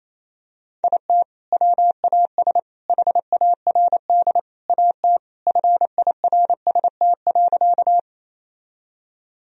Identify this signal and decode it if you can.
Morse code, heard as a keyed tone, about 28 wpm, 700 Hz. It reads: IT WAS HARD AT FIRST.
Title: Sentences from Top 1000 Words - Review Part 1 - 28wpm